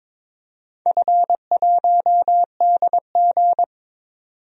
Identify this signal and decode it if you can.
Morse code, heard as a keyed tone, about 22 wpm, 700 Hz. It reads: F1DG
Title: DXCC-100 Call Signs - 4 Characters - 22wpm